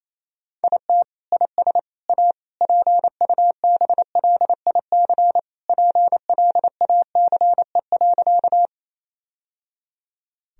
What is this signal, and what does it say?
Morse code, heard as a keyed tone, about 28 wpm, 700 Hz. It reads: IT IS A PUBLIC PLACE.
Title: Sentences from Top 1000 Words - Review Part 3 - 28wpm